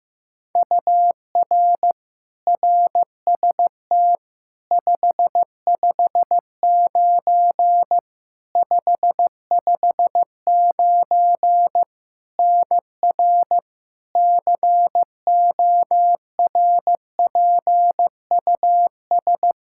Morse code, heard as a keyed tone, about 16 wpm, 700 Hz. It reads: UR RST 559 559 NR CORPUS
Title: K1USN SST QSO Practice 15wpm